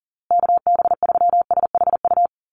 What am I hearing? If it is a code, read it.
Morse code, heard as a keyed tone, about 40 wpm, 700 Hz. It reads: KB3SHU
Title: USA Call Signs Repeated 3x 40wpm